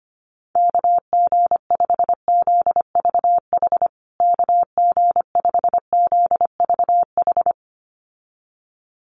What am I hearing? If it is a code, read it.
Morse code, heard as a keyed tone, about 25 wpm, 700 Hz. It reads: KG5ZVH KG5ZVH
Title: USA Call Signs Repeated 2x 25wpm